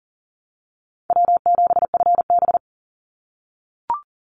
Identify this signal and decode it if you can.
Morse code, heard as a keyed tone, about 40 wpm, 700 Hz. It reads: W7FB
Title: Call Signs - 40wpm